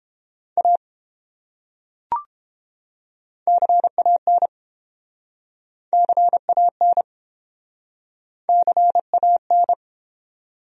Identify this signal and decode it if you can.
Morse code, heard as a keyed tone, about 29 wpm, 700 Hz. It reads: A CAN CAN CAN